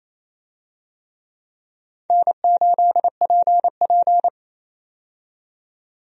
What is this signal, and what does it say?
Morse code, heard as a keyed tone, about 28 wpm, 700 Hz. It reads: N8PP
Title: USA Call Signs - 4 Characters - 28wpm